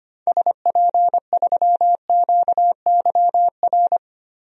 Morse code, heard as a keyed tone, about 25 wpm, 700 Hz. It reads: SP3QYR
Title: DXCC-100 Call Signs - 6 Characters - Repeated 3x - 25wpm